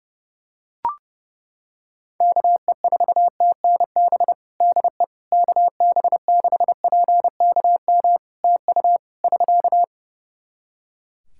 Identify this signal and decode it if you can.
Morse code, heard as a keyed tone, about 30 wpm, 700 Hz. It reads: KE4TNB DE KB6PKM TU <SK>
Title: Practice QSO - Exchange 4 - Ending - 30wpm